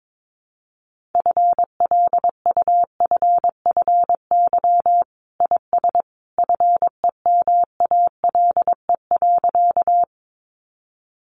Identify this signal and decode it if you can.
Morse code, heard as a keyed tone, about 22 wpm, 700 Hz. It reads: FLUFFY IS FEMALE.
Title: Easy Phrases 22wpm